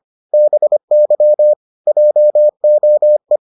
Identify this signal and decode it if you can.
Morse code, heard as a keyed tone, about 25 wpm, 600 Hz. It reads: BY JOE